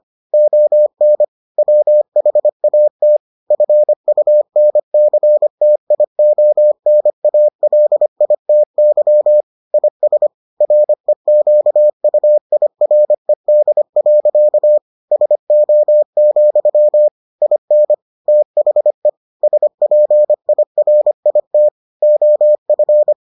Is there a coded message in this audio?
Yes: Morse code from 0.3 to 23.1 s, heard as a keyed tone, about 25 wpm, 600 Hz. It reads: ON WHAT FUNCTIONALITY IS REQUIRED. SO, IN THE SPIRIT OF